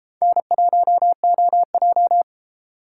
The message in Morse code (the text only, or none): N1OJ